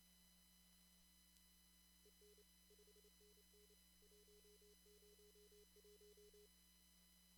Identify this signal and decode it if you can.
Morse code, heard as a keyed tone, about 28 wpm, 420 Hz. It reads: EN 5NN 101
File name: CW / kt5j wpx cw-op-k5pi